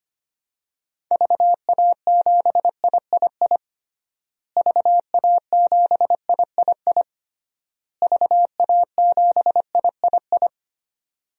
Morse code, heard as a keyed tone, about 25 wpm, 700 Hz. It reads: VA7III VA7III VA7III
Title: DXCC-100 Call Signs - 6 Characters - Repeated 3x - 25wpm